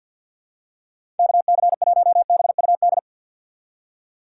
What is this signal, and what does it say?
Morse code, heard as a keyed tone, about 52 wpm, 700 Hz. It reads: KC1BUD
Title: USA Call Signs - 6x Character Spacing - 50wpm